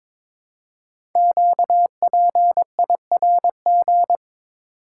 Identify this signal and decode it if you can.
Morse code, heard as a keyed tone, about 22 wpm, 700 Hz. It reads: QPIRG
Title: Code Groups - 5 Letters - 22wpm